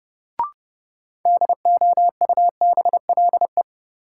Morse code, heard as a keyed tone, about 30 wpm, 700 Hz. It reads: DOUBLE
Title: Sentences from Top 1400 Words - New Word Drill Alt - 30wpm